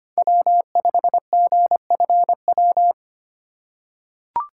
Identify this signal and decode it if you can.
Morse code, heard as a keyed tone, about 25 wpm, 700 Hz. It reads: W5GFW
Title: USA Call Signs - 5 Characters - 2x Character Spacing - 25wpm